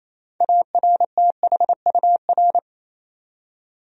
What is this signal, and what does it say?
Morse code, heard as a keyed tone, about 28 wpm, 700 Hz. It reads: ARTHUR
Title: USA Common Names - 28wpm